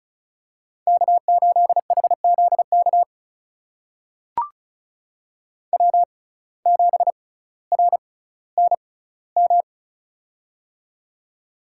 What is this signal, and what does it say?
Morse code, heard as a keyed tone, about 34 wpm, spaced as for 18 wpm, 700 Hz. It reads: K8HZK W7RNM
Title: USA Call Signs - 5 Characters - 6x Character Spacing - 35wpm